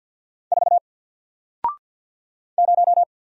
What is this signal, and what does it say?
Morse code, heard as a keyed tone, about 50 wpm, 700 Hz. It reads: 4 0
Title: Single Character (Letter:Number:Punctuation:Prosign) 50wpm